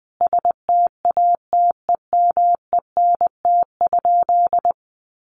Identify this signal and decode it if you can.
Morse code, heard as a keyed tone, about 20 wpm, 700 Hz. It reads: STATEMENT?
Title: Sentences from Top 1200 Words 20wpm